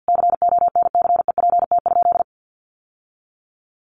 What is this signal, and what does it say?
Morse code, heard as a keyed tone, about 50 wpm, 700 Hz. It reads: CONCEPT?
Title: Sentences from Top 1100 Words 50wpm